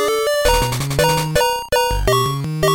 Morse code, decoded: none